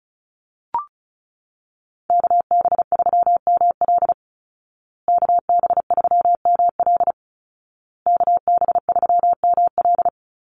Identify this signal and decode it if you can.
Morse code, heard as a keyed tone, about 34 wpm, 700 Hz. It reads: KB3ML KB3ML KB3ML
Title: Call Signs Repeated 2x - 35wpm